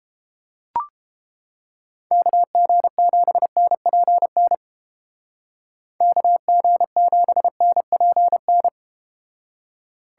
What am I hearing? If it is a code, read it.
Morse code, heard as a keyed tone, about 32 wpm, 700 Hz. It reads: KG7NPN KG7NPN